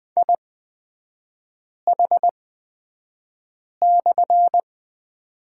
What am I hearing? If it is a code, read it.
Morse code, heard as a keyed tone, about 20 wpm, spaced as for 5 wpm, 690 Hz. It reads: IH/
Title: 229 Lesson 37e - Letter x - Three Characters 20wpm